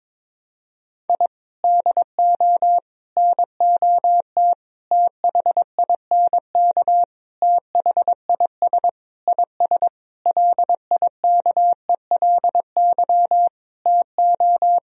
Morse code, heard as a keyed tone, about 22 wpm, 700 Hz. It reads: I DO NOT THINK THIS IS LIKELY TO
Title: Sentences from Top 1000 Words - Review Part 4 - 22wpm